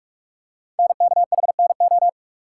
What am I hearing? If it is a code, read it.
Morse code, heard as a keyed tone, about 44 wpm, 690 Hz. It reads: NKHNO
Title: Code Groups - 5 Letters - 45wpm